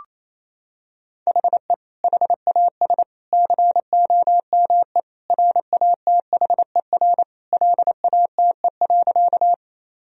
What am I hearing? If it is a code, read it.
Morse code, heard as a keyed tone, about 28 wpm, 700 Hz. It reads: HE HAS COME RATHER LATE.
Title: Sentences from Top 1000 Words - Review Part 4 - 28wpm